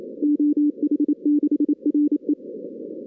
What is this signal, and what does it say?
Morse code, heard as a keyed tone, about 27 wpm, 300 Hz. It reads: OH6RE
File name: 2016 arrl dx cw / 2016-ARRL-DX-CW-K5TR- - 0-41